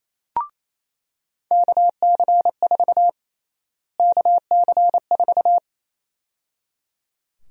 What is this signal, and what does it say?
Morse code, heard as a keyed tone, about 28 wpm, 700 Hz. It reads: KC4 KC4
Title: USA Call Sign Prefixes - Repeated 2x - 28wpm